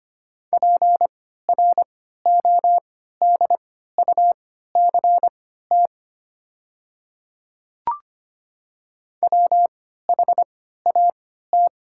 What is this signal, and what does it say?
Morse code, heard as a keyed tone, about 25 wpm, spaced as for 14 wpm, 700 Hz. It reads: PRODUCT WHAT
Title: 3x Character Spacing - Top 500 Words - 25wpm